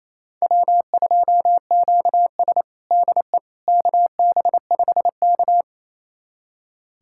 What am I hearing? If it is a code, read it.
Morse code, heard as a keyed tone, about 28 wpm, 700 Hz. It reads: W2QS DE KB5K
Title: USA Callsign DE Callsign - 4 Characters - 28wpm